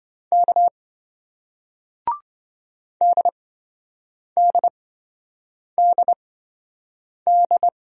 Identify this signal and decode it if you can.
Morse code, heard as a keyed tone, about 26 wpm, spaced as for 7 wpm, 700 Hz. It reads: K DDDD